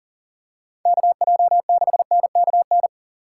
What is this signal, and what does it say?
Morse code, heard as a keyed tone, about 41 wpm, 700 Hz. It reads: KJ6NKN
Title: USA Call Signs - 6 Characters - 4x Character Spacing - 40wpm